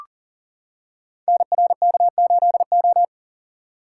Morse code, heard as a keyed tone, about 39 wpm, 690 Hz. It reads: NRK8O
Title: Code Groups - 5 Letter:Numbers - 40wpm